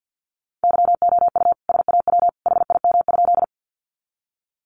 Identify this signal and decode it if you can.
Morse code, heard as a keyed tone, about 50 wpm, 700 Hz. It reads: YOU SAW HIM?
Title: Sentences from Top 300 Words 50wpm